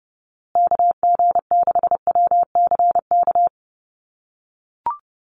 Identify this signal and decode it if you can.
Morse code, heard as a keyed tone, about 30 wpm, 700 Hz. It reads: KG6WCK
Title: USA Call Signs Repeated 3x 30wpm